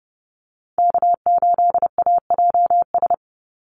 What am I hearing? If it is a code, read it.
Morse code, heard as a keyed tone, about 30 wpm, 700 Hz. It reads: K8AJS
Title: Call Signs - 30wpm